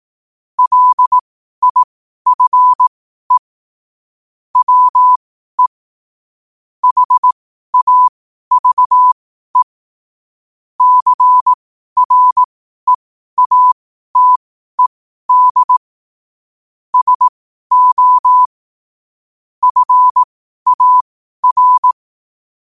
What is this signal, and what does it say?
Morse code, heard as a keyed tone, about 18 wpm, spaced as for 12 wpm, 1000 Hz. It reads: LIFE WE HAVE CREATED SO FAR
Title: 12 WPM morse code quotes for Fri, 01 Mar 2024